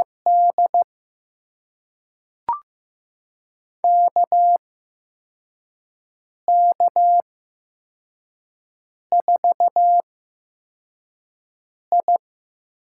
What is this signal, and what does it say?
Morse code, heard as a keyed tone, about 15 wpm, spaced as for 4 wpm, 700 Hz. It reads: D KK4I
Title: USA Call Signs - 6 Characters - 8x Character Spacing - 15wpm